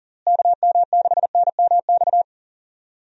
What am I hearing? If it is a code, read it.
Morse code, heard as a keyed tone, about 38 wpm, 700 Hz. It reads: KM6NMX
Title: USA Call Signs - 6 Characters - Repeated 2x - 40wpm